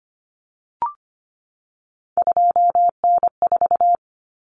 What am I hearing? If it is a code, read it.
Morse code, heard as a keyed tone, about 25 wpm, 700 Hz. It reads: 2N4
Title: Common Cut Numbers - 3-Digit 25wpm